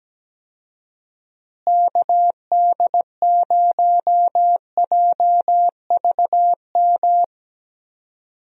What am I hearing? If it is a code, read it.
Morse code, heard as a keyed tone, about 17 wpm, 700 Hz. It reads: KD0JVM
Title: USA Call Signs - 6 Characters - Repeated 2x - 17wpm